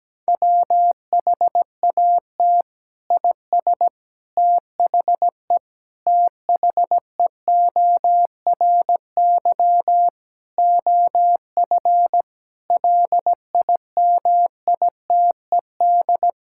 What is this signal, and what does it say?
Morse code, heard as a keyed tone, about 17 wpm, 700 Hz. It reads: WHAT IS THE THEORY OF LIMITED
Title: Sentences from Top 1000 Words - Review Part 6 - 17wpm